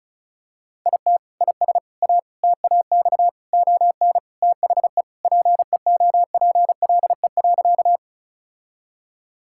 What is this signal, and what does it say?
Morse code, heard as a keyed tone, about 36 wpm, 700 Hz. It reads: IT IS A TAX ON THE PEOPLE.
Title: Sentences from Top 1000 Words - Review Part 5 - 35wpm